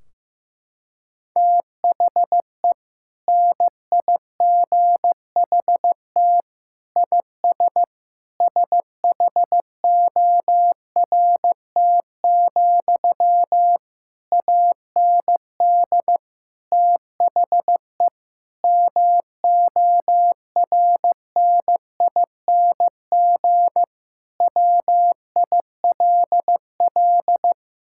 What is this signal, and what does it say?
Morse code, heard as a keyed tone, about 15 wpm, 700 Hz. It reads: THE NIGHT IS SHORT, AND THE MORNING WILL
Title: Sentences from Top 1600 Words 15wpm